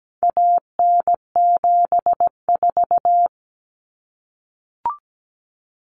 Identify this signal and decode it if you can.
Morse code, heard as a keyed tone, about 17 wpm, 700 Hz. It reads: AN74
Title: Common Cut Numbers - 4-Digit 17wpm